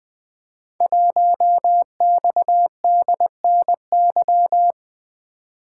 Morse code, heard as a keyed tone, about 20 wpm, 700 Hz. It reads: 1XDNY
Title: Code Groups - 5 Letter:Numbers - 20wpm